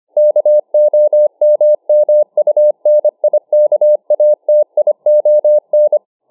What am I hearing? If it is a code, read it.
Morse code, heard as a keyed tone, about 25 wpm, 600 Hz. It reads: KOMMUNIKATION